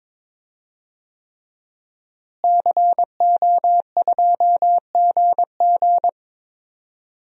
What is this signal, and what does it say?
Morse code, heard as a keyed tone, about 22 wpm, 700 Hz. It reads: CO2GG
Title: DXCC-100 Call Signs - 5 Characters - Repeated 3x - 22wpm